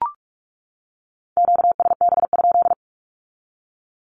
Morse code, heard as a keyed tone, about 44 wpm, 700 Hz. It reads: QSB?
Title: Q-Codes 45wpm